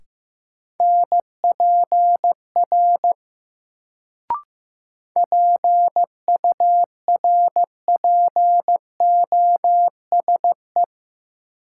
Morse code, heard as a keyed tone, about 15 wpm, 700 Hz. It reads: NPR PURPOSE
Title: Sets of 5 Words - New Word Drill - 15wpm